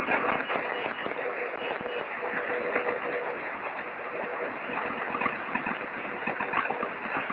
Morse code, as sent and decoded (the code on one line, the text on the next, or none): none